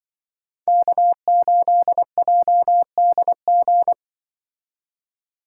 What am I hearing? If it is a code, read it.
Morse code, heard as a keyed tone, about 24 wpm, 700 Hz. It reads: K8JDG
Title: Speed Racing - USA 1x3 Call Signs 20wpm